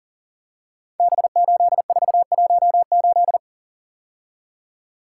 Morse code, heard as a keyed tone, about 38 wpm, 690 Hz. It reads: B8418
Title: Code Groups - 5 Letter:Numbers - 40wpm